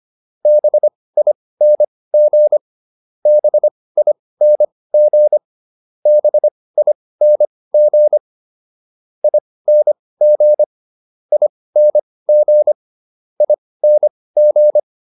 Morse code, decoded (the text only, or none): BING BING BING ING ING ING